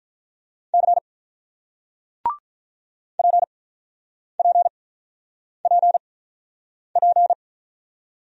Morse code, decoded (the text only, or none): C PPPP